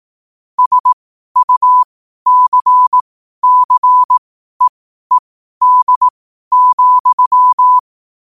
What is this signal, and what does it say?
Morse code, heard as a keyed tone, about 18 wpm, spaced as for 12 wpm, 1000 Hz. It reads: SUCCEED,
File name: qotdfiles / 12wpm